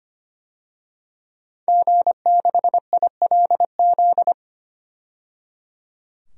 Morse code, heard as a keyed tone, about 25 wpm, 700 Hz. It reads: G6ILZ
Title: DXCC-100 Call Signs - 5 Characters - 25wpm